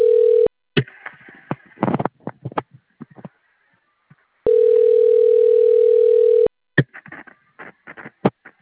Ringback tone (North America), 440 + 480 Hz.